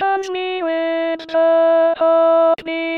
There is no code.